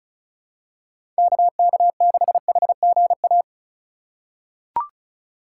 Morse code, heard as a keyed tone, about 36 wpm, 700 Hz. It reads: KK6HGA
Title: USA Call Signs - 6 Characters - 4x Character Spacing - 35wpm